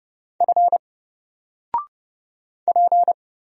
Morse code, heard as a keyed tone, about 31 wpm, 700 Hz. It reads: F P
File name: Single-Letter-Number-30wpm